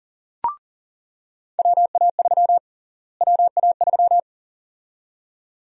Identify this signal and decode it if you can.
Morse code, heard as a keyed tone, about 37 wpm, 700 Hz. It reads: WA3 WA3
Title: USA Call Sign Prefixes - Repeated 2x - 40wpm